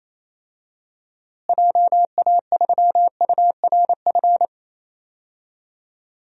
Morse code, heard as a keyed tone, about 28 wpm, 700 Hz. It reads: JA3URF
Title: DXCC-100 Call Signs - 6 Characters - Repeated 2x - 28wpm